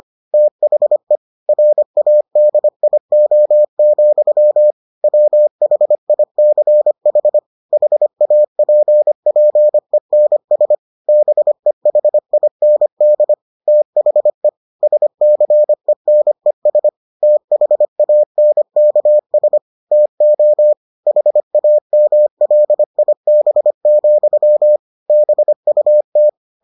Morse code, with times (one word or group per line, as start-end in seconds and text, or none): 0.3-1.2 THE
1.5-4.7 RADIO,
5.0-7.4 WHICH
7.7-10.8 HAPPENS
11.1-13.3 BEHIND
13.7-14.5 THE
14.8-16.9 SCENES
17.2-19.6 THANKS
19.9-20.7 TO
21.1-24.8 HAMLIB,
25.1-26.3 BUT